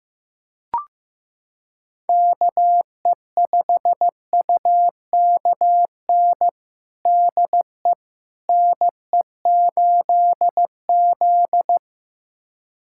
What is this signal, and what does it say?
Morse code, heard as a keyed tone, about 15 wpm, 700 Hz. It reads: KE5UKN DE NE8Z
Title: USA Callsign DE Callsign 15wpm